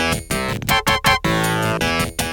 There is no code.